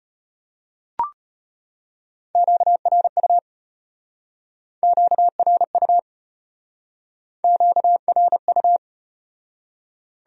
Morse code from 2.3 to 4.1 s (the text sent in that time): QRU